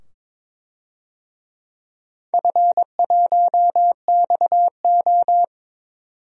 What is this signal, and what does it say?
Morse code, heard as a keyed tone, about 22 wpm, 700 Hz. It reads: F1XO
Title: DXCC-100 Call Signs - 4 Characters - 22wpm